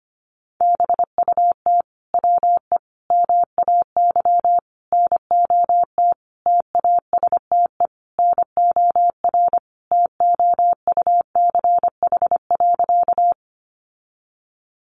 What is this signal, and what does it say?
Morse code, heard as a keyed tone, about 25 wpm, 700 Hz. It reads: BUT WE MAY NOT TASTE NOR TOUCH.
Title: Sentences from Top 1500 Words 25wpm